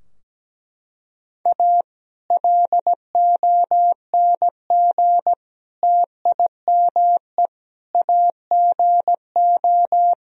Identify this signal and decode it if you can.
Morse code, heard as a keyed tone, about 17 wpm, 700 Hz. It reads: A LONG TIME AGO
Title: Sentences from Top 1000 Words - Review Part 3 - 17wpm